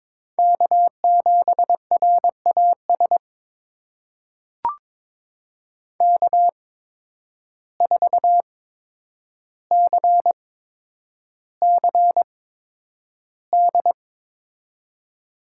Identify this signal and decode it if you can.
Morse code, heard as a keyed tone, about 22 wpm, spaced as for 9 wpm, 700 Hz. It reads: K7RAS K4CCD
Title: USA Call Signs - 5 Characters - 8x Character Spacing - 22wpm